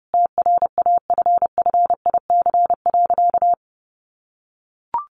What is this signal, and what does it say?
Morse code, heard as a keyed tone, about 30 wpm, 700 Hz. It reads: TRAFFIC.
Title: Sentences from Top 1300 Words 30wpm